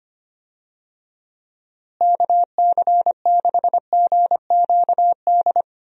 Morse code, heard as a keyed tone, about 25 wpm, 700 Hz. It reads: KC6GQD
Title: USA Call Signs - 6 Characters - Repeated 2x - 25wpm